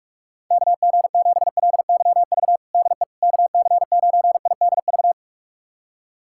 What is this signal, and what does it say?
Morse code, heard as a keyed tone, about 46 wpm, 700 Hz. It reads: KG7LYV DE KC9IDV
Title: USA Callsign DE Callsign - 6 Characters - 45wpm